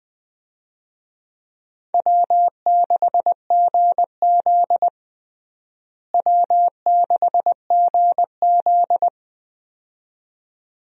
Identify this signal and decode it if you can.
Morse code, heard as a keyed tone, about 21 wpm, 700 Hz. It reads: W6GZ W6GZ
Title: USA Call Signs - 4 Characters - Repeated 2x - 20wpm